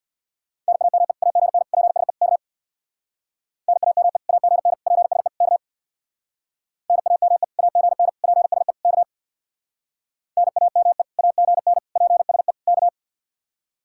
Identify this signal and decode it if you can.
Morse code, heard as a keyed tone, about 66 wpm, 700 Hz. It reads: NAME AGN PSE K NAME AGN PSE K NAME AGN PSE K NAME AGN PSE K